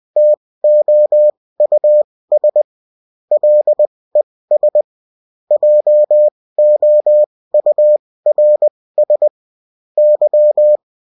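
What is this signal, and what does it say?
Morse code, heard as a keyed tone, about 20 wpm, spaced as for 16 wpm, 600 Hz. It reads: TOUS LES JOURS Y
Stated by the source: Bulletin in text and CW form.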